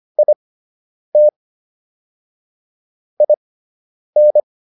Morse code, heard as a keyed tone, about 25 wpm, spaced as for 9 wpm, 600 Hz. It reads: IT IN